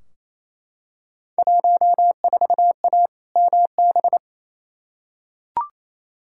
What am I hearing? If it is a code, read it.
Morse code, heard as a keyed tone, about 28 wpm, 700 Hz. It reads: 14A MB
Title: Field Day Exchanges Encore - 28wpm